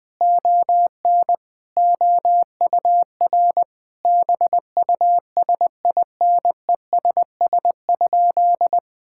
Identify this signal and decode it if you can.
Morse code, heard as a keyed tone, about 20 wpm, 700 Hz. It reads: ON OUR BUSINESS?
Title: Sentences from Top 1000 Words - Review Part 6 - 20wpm